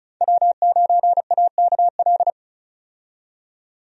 Morse code, heard as a keyed tone, about 36 wpm, 700 Hz. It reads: W9AKL
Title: USA Call Signs - 5 Characters - 35wpm